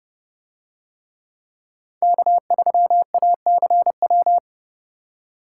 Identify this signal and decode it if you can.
Morse code, heard as a keyed tone, about 29 wpm, 700 Hz. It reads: K3ACW